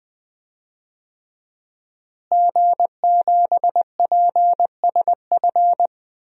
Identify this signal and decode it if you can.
Morse code, heard as a keyed tone, about 20 wpm, 700 Hz. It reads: G7PSF
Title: DXCC-100 Call Signs - 5 Characters - Repeated 3x - 20wpm